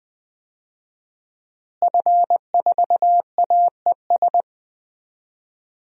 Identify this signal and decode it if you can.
Morse code, heard as a keyed tone, about 20 wpm, 700 Hz. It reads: F4AES
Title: DXCC-100 Call Signs - 5 Characters - Repeated 2x - 20wpm